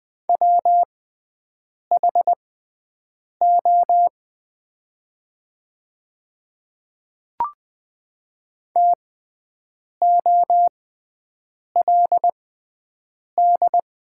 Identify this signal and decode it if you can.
Morse code, heard as a keyed tone, about 20 wpm, spaced as for 7 wpm, 700 Hz. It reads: WHO TOLD
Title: 6x Character Spacing - Top 500 Words - 20wpm